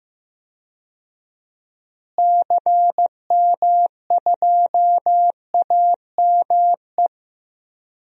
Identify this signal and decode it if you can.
Morse code, heard as a keyed tone, about 15 wpm, 700 Hz. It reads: CM2AME
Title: DXCC-100 Call Signs - 6 Characters - Repeated 2x - 15wpm